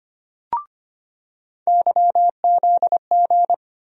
Morse code, heard as a keyed tone, about 25 wpm, 700 Hz. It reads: YZG
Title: USA Call Sign Postfixes - 3 Letters 25wpm